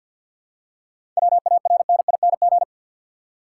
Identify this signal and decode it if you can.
Morse code, heard as a keyed tone, about 50 wpm, 700 Hz. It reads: WARNING
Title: Seven Letter Words - 50wpm